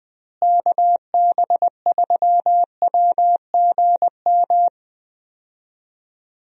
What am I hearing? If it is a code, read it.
Morse code, heard as a keyed tone, about 21 wpm, 700 Hz. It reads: KB3WGM
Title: USA Call Signs - 6 Characters - Repeated 3x - 20wpm